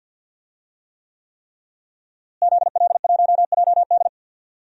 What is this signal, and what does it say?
Morse code, heard as a keyed tone, about 49 wpm, 700 Hz. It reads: ZL1JD
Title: DXCC-100 Call Signs - 5 Characters - Repeated 3x - 50wpm